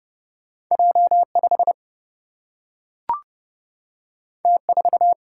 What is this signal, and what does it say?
Morse code, heard as a keyed tone, about 30 wpm, 700 Hz. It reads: J5 T4
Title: Single-Letter Single-Number 30wpm